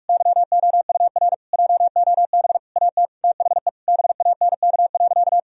Morse code, heard as a keyed tone, about 44 wpm, 700 Hz. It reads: YOUR JOB AT THE BANK.
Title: Sentences from Top 1000 Words - Review Part 4 - 45wpm